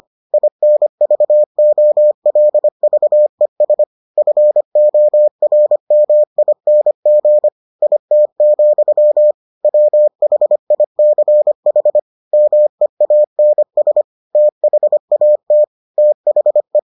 Morse code, heard as a keyed tone, about 25 wpm, 600 Hz. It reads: INVOLVES FORMING IT, WHICH MEANS THAT THE